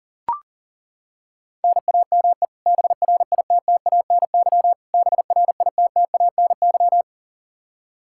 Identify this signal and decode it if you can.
Morse code, heard as a keyed tone, about 37 wpm, 700 Hz. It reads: NAME BRITTANY BRITTANY
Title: Practice QSO - Exchange 1 - Beginning - 40wpm